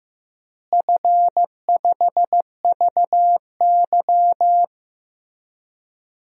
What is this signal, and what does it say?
Morse code, heard as a keyed tone, about 15 wpm, 700 Hz. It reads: F5VY
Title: DXCC-100 Call Signs - 4 Characters - Repeated 3x - 15wpm